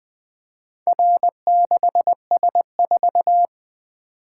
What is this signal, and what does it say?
Morse code, heard as a keyed tone, about 20 wpm, 700 Hz. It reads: R6S4
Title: Four Letter-Number Combinations - 20wpm